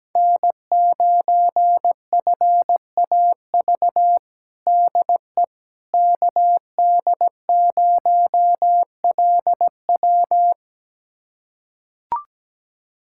Morse code, heard as a keyed tone, about 17 wpm, 700 Hz. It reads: N9FAV DE KD0LW
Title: USA Callsign DE Callsign - 5 Characters - 17wpm